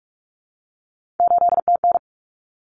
Morse code, heard as a keyed tone, about 44 wpm, 700 Hz. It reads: 8TN
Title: Common Cut Numbers - 3-Digit 45wpm